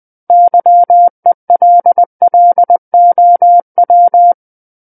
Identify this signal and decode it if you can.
Morse code, heard as a keyed tone, about 20 wpm, 700 Hz. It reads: YELLOW
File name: 20 WPM